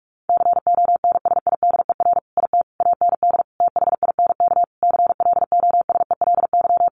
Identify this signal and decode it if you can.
Morse code, heard as a keyed tone, about 44 wpm, 700 Hz. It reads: CONSIDER IT AND THINK CLOSELY
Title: Sentences from Top 1400 Words 45wpm